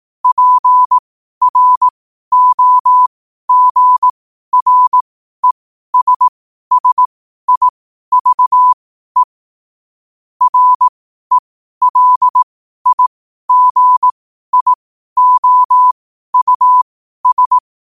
Morse code, heard as a keyed tone, about 18 wpm, spaced as for 12 wpm, 1000 Hz. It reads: PROGRESSIVE RELIGIOUS
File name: qotdfiles / 12wpm